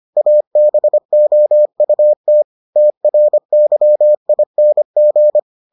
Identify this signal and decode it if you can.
Morse code, heard as a keyed tone, about 25 wpm, 600 Hz. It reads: ABOUT TRYING